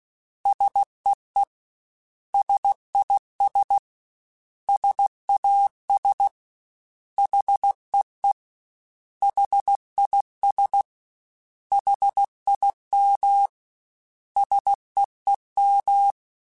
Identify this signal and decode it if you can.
Morse code, heard as a keyed tone, about 16 wpm, 800 Hz. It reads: SEE SIS SAS HEE HIS HIM SEEM
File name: S vs H / S vs H mp3s / S vs H @15